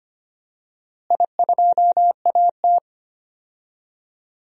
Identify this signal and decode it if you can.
Morse code, heard as a keyed tone, about 25 wpm, 700 Hz. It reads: I2AT
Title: DXCC-100 Call Signs - 4 Characters - Repeated 2x - 25wpm